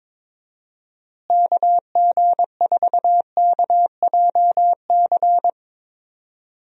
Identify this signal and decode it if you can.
Morse code, heard as a keyed tone, about 22 wpm, 700 Hz. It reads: KG4KJC
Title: USA Call Signs - 6 Characters - Repeated 2x - 22wpm